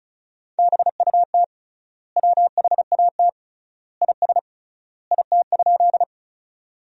Morse code, heard as a keyed tone, about 36 wpm, 700 Hz. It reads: BUT WHAT IS IT?
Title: 3x Word-Spacing - Sentences from Top 100 Words - 35wpm